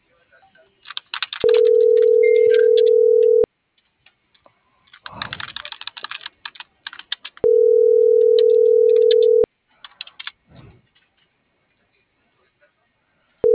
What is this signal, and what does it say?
Ringback tone (North America), 440 + 480 Hz.